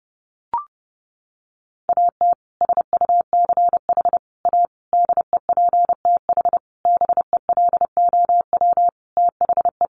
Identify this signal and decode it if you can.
Morse code, heard as a keyed tone, about 30 wpm, 700 Hz. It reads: AT SUCH A DEPTH BELOW THE
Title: Sentences from Top 1200 Words 30wpm